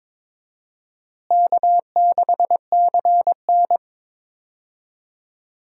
Morse code, heard as a keyed tone, about 22 wpm, 700 Hz. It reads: K6CN
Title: USA Call Signs - 4 Characters - Repeated 3x - 22wpm